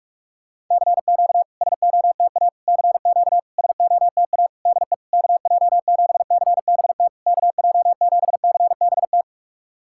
Morse code, heard as a keyed tone, about 44 wpm, 700 Hz. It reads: CQ SOTA CQ SOTA DE KJ7CBT KJ7CBT
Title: SOTA QSO Practice 45wpm